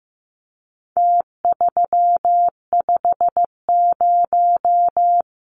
Morse code, heard as a keyed tone, about 15 wpm, 700 Hz. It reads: T350
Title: Numbers - Contest Style - 4-Digit 15wpm